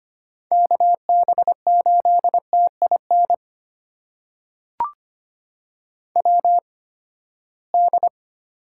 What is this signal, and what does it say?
Morse code, heard as a keyed tone, about 25 wpm, spaced as for 17 wpm, 700 Hz. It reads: KB8TIN WD
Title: USA Call Signs - 6 Characters - 8x Character Spacing - 25wpm